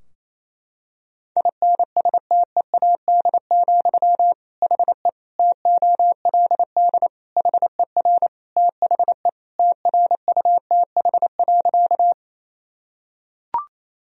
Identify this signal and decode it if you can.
Morse code, heard as a keyed tone, about 28 wpm, 700 Hz. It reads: INSTEAD, HE TOLD HER THE TRUTH.
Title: Sentences from Top 1000 Words - Review Part 4 - 28wpm